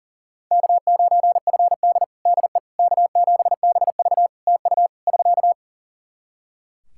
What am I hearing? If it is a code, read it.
Morse code, heard as a keyed tone, about 40 wpm, 700 Hz. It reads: K9FD DE K7BV TU <SK>
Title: Practice QSO - Exchange 4 - Ending - 40wpm